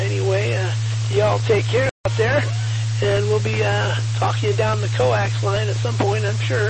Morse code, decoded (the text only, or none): none